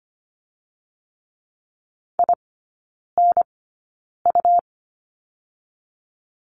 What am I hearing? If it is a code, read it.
Morse code, heard as a keyed tone, about 25 wpm, 700 Hz. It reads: I N U
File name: Lesson 17e - Letter u - Three Characters 25wpm